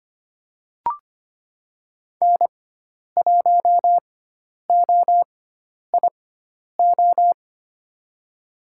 Morse code, heard as a keyed tone, about 25 wpm, spaced as for 10 wpm, 700 Hz. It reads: N1OIO
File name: Lesson 16g - Letter c - Call Signs 25wpm